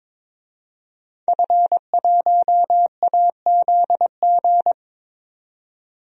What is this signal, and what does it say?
Morse code, heard as a keyed tone, about 22 wpm, 700 Hz. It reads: F1AZG
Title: DXCC-100 Call Signs - 5 Characters - 22wpm